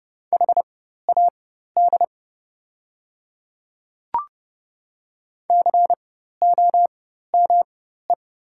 Morse code, heard as a keyed tone, about 30 wpm, spaced as for 14 wpm, 700 Hz. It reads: HAD COME
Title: 4x Character Spacing - Top 100 Words - 30wpm